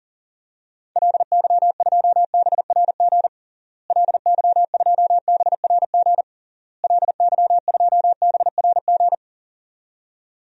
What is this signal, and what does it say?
Morse code, heard as a keyed tone, about 40 wpm, 700 Hz. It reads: LY2BRG LY2BRG LY2BRG
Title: DXCC-100 Call Signs - 6 Characters - Repeated 3x - 40wpm